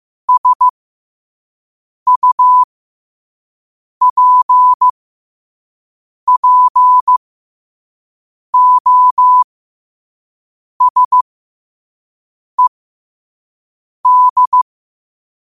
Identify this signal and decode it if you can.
Morse code, heard as a keyed tone, about 15 wpm, spaced as for 5 wpm, 1000 Hz. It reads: SUPPOSED
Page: New quotes every day in morse code at 5 Words per minute.